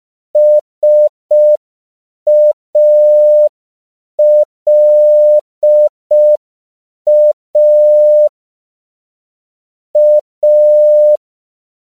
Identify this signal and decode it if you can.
Morse code, heard as a keyed tone, about 5 wpm, 600 Hz. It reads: SALA A
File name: CW / 5PPM